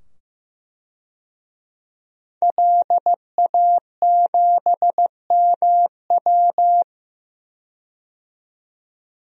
Morse code, heard as a keyed tone, about 15 wpm, 700 Hz. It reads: LA7MW
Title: DXCC-100 Call Signs - 5 Characters - 15wpm